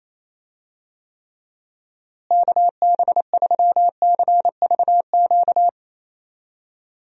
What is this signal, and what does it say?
Morse code, heard as a keyed tone, about 28 wpm, 700 Hz. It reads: KB3CVQ
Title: USA Call Signs - 6 Characters - Repeated 3x - 28wpm